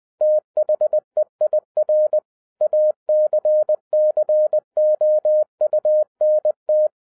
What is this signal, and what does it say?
Morse code, heard as a keyed tone, about 20 wpm, 610 Hz. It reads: THEIR ACCOUNT